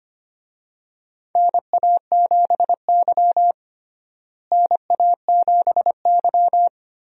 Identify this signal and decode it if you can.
Morse code, heard as a keyed tone, about 25 wpm, 700 Hz. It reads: NA7Y NA7Y
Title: USA Call Signs - 4 Characters - Repeated 3x - 25wpm